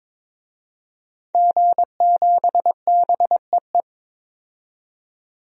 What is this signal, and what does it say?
Morse code, heard as a keyed tone, about 22 wpm, 700 Hz. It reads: G7BEE
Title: DXCC-100 Call Signs - 5 Characters - 22wpm